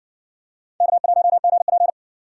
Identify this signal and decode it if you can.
Morse code, heard as a keyed tone, about 70 wpm, 700 Hz. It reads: K1GP